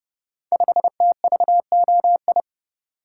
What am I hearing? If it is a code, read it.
Morse code, heard as a keyed tone, about 30 wpm, 700 Hz. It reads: 5TVOI
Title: Code Groups - 5 Letter:Numbers - 30wpm